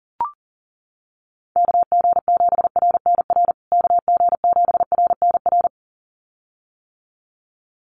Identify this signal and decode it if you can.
Morse code, heard as a keyed tone, about 40 wpm, 700 Hz. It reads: KG7RNR KG7RNR
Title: USA Call Signs Repeated 2x 40wpm